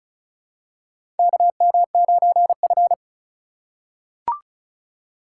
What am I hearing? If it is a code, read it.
Morse code, heard as a keyed tone, about 36 wpm, 700 Hz. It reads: KM9F
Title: USA Call Signs - 4 Characters - 4x Character Spacing - 35wpm